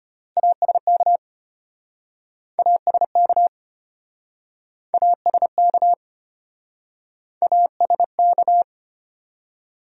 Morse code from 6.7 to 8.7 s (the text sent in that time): ASK